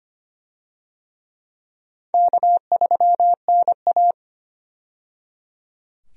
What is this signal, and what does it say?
Morse code, heard as a keyed tone, about 25 wpm, 700 Hz. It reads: K3NA
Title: USA Call Signs - 4 Characters - 25wpm